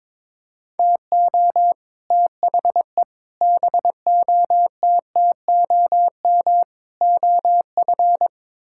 Morse code, heard as a keyed tone, about 22 wpm, 700 Hz. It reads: TO THE BOTTOM OF
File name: practice / 22wpm / Build A Phrase - 5-Word Phrases - 22wpm